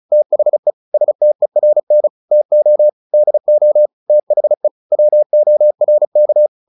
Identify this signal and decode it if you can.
Morse code, heard as a keyed tone, about 34 wpm, 600 Hz. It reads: THE STERN TO DO THE WORK
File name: the old man and the sea 2 35wpm